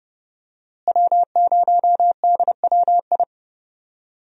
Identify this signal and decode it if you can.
Morse code, heard as a keyed tone, about 30 wpm, 700 Hz. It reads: W0DWI
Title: USA Call Signs - 5 Characters - Repeated 2x - 30wpm